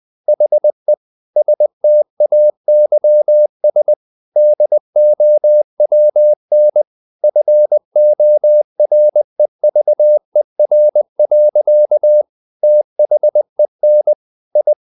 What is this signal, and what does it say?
Morse code, heard as a keyed tone, about 20 wpm, 600 Hz. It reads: HE STAYS DOWN FOREVER. THEN I
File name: the old man and the sea / the old man and the sea part 1 20wpm